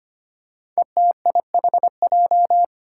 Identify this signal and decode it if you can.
Morse code, heard as a keyed tone, about 25 wpm, 700 Hz. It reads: ETIHJ
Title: Five Challenging Letter-Number Combinations 25wpm